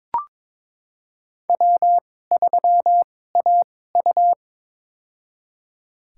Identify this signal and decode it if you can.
Morse code, heard as a keyed tone, about 22 wpm, spaced as for 16 wpm, 700 Hz. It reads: W3AU
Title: USA Call Signs - 2x Character Spacing - 22wpm